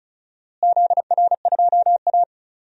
Morse code, heard as a keyed tone, about 34 wpm, 700 Hz. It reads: ZR2A
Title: DXCC-100 Call Signs - 4 Characters - Repeated 3x - 35wpm